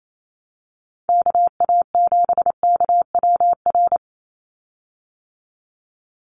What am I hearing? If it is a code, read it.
Morse code, heard as a keyed tone, about 28 wpm, 700 Hz. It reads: KA7KWR
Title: USA Call Signs 28wpm